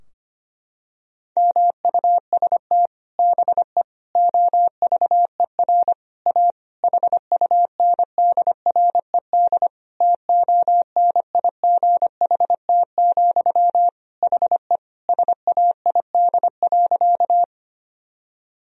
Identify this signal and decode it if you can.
Morse code, heard as a keyed tone, about 25 wpm, 700 Hz. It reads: MUST BE OVER A HUNDRED TONIGHT, HE SAID.
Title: Sentences from Top 1000 Words - Review Part 2 - 25wpm